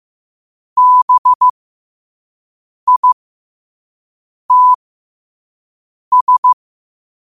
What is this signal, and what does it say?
Morse code, heard as a keyed tone, about 15 wpm, spaced as for 5 wpm, 1000 Hz. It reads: BITS
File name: qotdfiles / 5wpm